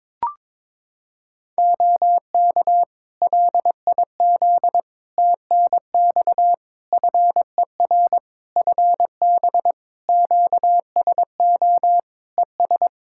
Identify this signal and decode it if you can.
Morse code, heard as a keyed tone, about 22 wpm, 700 Hz. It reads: OK LIZ TNX FER FB QSO ES
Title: K1USN SST QSO Practice 22wpm